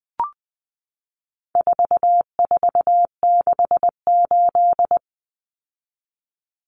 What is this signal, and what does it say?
Morse code, heard as a keyed tone, about 20 wpm, 700 Hz. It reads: 4468
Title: Numbers - 4-Digit 20wpm